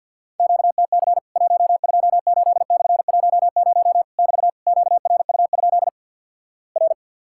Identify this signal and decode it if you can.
Morse code, heard as a keyed tone, about 50 wpm, 700 Hz. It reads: QTC 128/10 <BT> QRU? R
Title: Worked All Europe DX Contest - QTC Exchange with Repeats - 50wpm